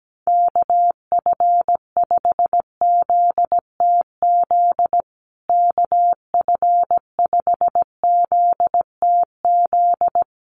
Morse code, heard as a keyed tone, about 17 wpm, 700 Hz. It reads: KF5ZTZ KF5ZTZ
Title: USA Call Signs Repeated 3x 17wpm